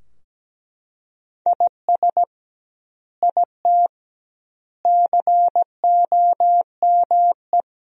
Morse code, heard as a keyed tone, about 17 wpm, 700 Hz. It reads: IS IT COME
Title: 2x Word-Spacing - Sentences from Top 500 Words - 17wpm